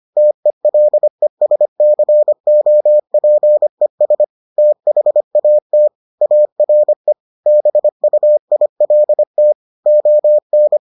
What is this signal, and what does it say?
Morse code, heard as a keyed tone, about 25 wpm, 600 Hz. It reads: TELESCOPES THAT ARE BUILT ON